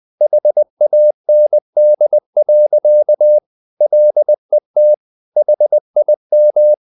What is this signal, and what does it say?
Morse code, heard as a keyed tone, about 20 wpm, 600 Hz. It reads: HAND. LET HIM
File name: the old man and the sea / the old man and the sea part 1 20wpm